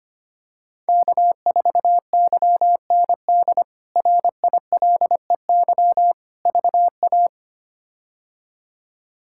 Morse code, heard as a keyed tone, about 25 wpm, 700 Hz. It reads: K4YND RILEY VA
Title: K1USN SST QSO Practice - 25wpm